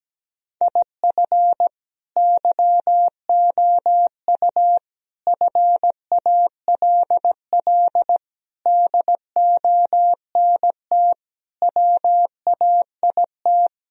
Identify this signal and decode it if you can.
Morse code, heard as a keyed tone, about 17 wpm, 700 Hz. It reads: IF YOU FALL DONT WAIT
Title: Sentences from Top 1000 Words - Review Part 2 - 17wpm